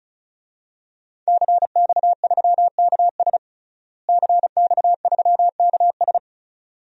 Morse code, heard as a keyed tone, about 36 wpm, 700 Hz. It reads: CX3KS CX3KS
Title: DXCC-100 Call Signs - 5 Characters - Repeated 2x - 35wpm